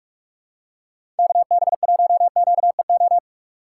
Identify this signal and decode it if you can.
Morse code, heard as a keyed tone, about 44 wpm, 700 Hz. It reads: KB1QEO
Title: USA Call Signs - 6 Characters - 6x Character Spacing - 45wpm